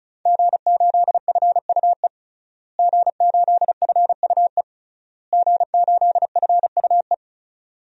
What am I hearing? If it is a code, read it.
Morse code, heard as a keyed tone, about 36 wpm, 700 Hz. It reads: G8FUE G8FUE G8FUE
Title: DXCC-100 Call Signs - 5 Characters - Repeated 3x - 35wpm